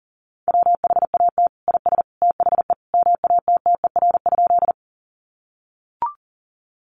Morse code, heard as a keyed tone, about 40 wpm, 700 Hz. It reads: WHAT IS THE MATTER?
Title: Sentences from Top 700 Words 40wpm